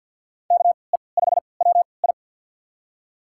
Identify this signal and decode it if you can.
Morse code, heard as a keyed tone, about 52 wpm, spaced as for 28 wpm, 700 Hz. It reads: KE5WI
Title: USA Call Signs - 5 Characters - 3x Character Spacing - 50wpm